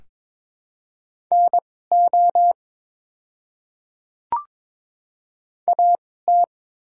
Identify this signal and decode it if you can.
Morse code, heard as a keyed tone, about 22 wpm, spaced as for 16 wpm, 700 Hz. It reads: NO AT
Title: 2x Character Spacing - Top 100 Words - 22wpm